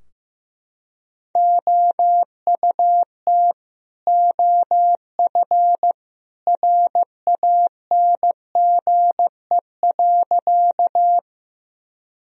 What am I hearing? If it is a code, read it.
Morse code, heard as a keyed tone, about 15 wpm, 700 Hz. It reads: OUT OF RANGE.
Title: Sentences from Top 1000 Words - Review Part 4 - 15wpm